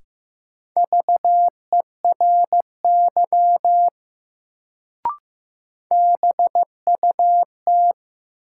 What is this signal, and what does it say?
Morse code, heard as a keyed tone, about 15 wpm, 700 Hz. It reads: VERY BUT
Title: Sets of 2 Words - New Word Drill - 15wpm